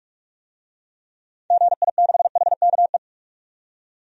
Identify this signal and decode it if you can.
Morse code, heard as a keyed tone, about 46 wpm, 700 Hz. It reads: GI6HKE
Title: DXCC-100 Call Signs - 6 Characters - 45wpm